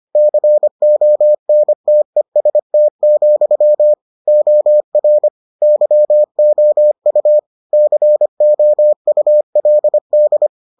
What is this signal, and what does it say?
Morse code, heard as a keyed tone, about 25 wpm, 600 Hz. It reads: CONTEST, OR YOU COULD